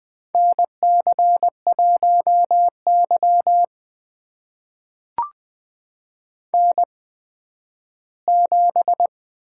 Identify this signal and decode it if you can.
Morse code, heard as a keyed tone, about 20 wpm, spaced as for 12 wpm, 700 Hz. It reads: NC1Y N7
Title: USA Call Signs - 4 Characters - 8x Character Spacing - 20wpm